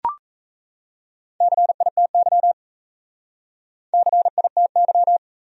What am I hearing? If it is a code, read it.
Morse code, heard as a keyed tone, about 36 wpm, 700 Hz. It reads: CITY CITY